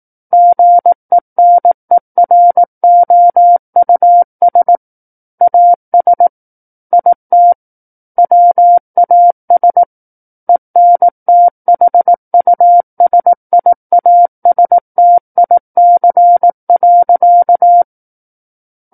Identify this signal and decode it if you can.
Morse code, heard as a keyed tone, about 18 wpm, 700 Hz. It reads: GENEROUS AS IT WAS ENTHUSIASTIC.
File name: 18 WPM